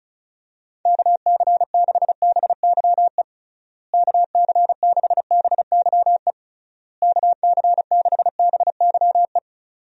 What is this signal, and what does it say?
Morse code, heard as a keyed tone, about 34 wpm, 700 Hz. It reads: KC6BYE KC6BYE KC6BYE
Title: USA Call Signs - 6 Characters - Repeated 3x - 35wpm